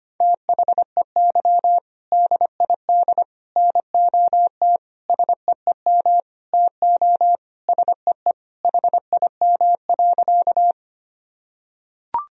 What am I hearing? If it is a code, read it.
Morse code, heard as a keyed tone, about 25 wpm, 700 Hz. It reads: THEY DID NOT SEEM TO SEE HIM.
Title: Sentences from Top 1000 Words - Review Part 1 - 25wpm